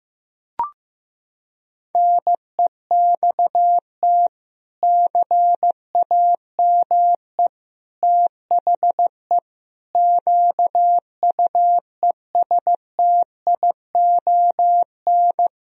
Morse code, heard as a keyed tone, about 15 wpm, 700 Hz. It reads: NEXT CAME THE QUESTION
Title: Sentences from Top 1000 Words - Review Part 4 - 15wpm